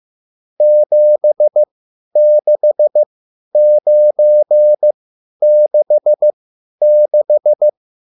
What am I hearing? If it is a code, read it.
Morse code, heard as a keyed tone, about 15 wpm, spaced as for 10 wpm, 600 Hz. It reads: 76966